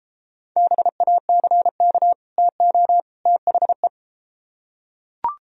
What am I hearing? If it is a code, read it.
Morse code, heard as a keyed tone, about 33 wpm, 700 Hz. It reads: BACK TO THE